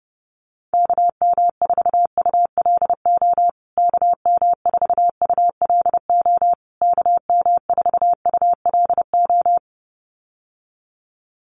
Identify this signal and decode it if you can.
Morse code, heard as a keyed tone, about 30 wpm, 700 Hz. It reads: KM4ULO KM4ULO KM4ULO
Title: USA Call Signs Repeated 3x 30wpm